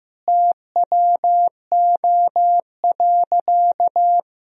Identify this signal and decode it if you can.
Morse code, heard as a keyed tone, about 16 wpm, 700 Hz. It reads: TWO.
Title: Sentences from Top 200 Words and Abbreviations 15wpm